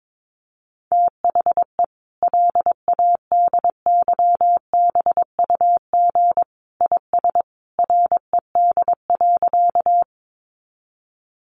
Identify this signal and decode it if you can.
Morse code, heard as a keyed tone, about 22 wpm, 700 Hz. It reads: THE LADYBUG IS RED.
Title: Additional Easy Phrases 22wpm